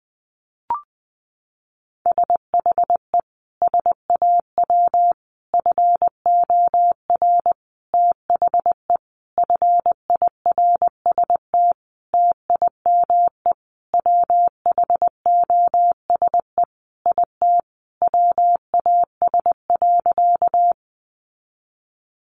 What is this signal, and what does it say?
Morse code, heard as a keyed tone, about 20 wpm, 700 Hz. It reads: SHE SAW FOR THE FIRST TIME WHOSE IT WAS.
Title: Sentences from Top 1200 Words 20wpm